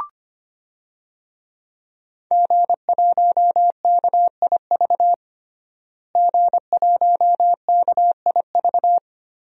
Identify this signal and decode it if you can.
Morse code, heard as a keyed tone, about 25 wpm, 700 Hz. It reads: G1KIV G1KIV
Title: DXCC-100 Call Signs - 5 Characters - Repeated 2x - 25wpm